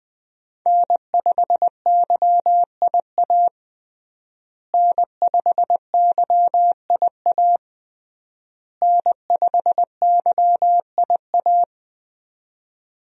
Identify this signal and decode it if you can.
Morse code, heard as a keyed tone, about 20 wpm, 700 Hz. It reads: N5YIA N5YIA N5YIA
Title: USA Call Signs - 5 Characters - Repeated 3x - 20wpm